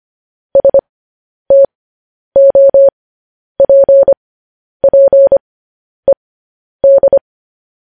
Morse code, heard as a keyed tone, about 25 wpm, spaced as for 10 wpm, 550 Hz. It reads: STOPPED
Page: news-headlines-in-morse-code-at-10-wpm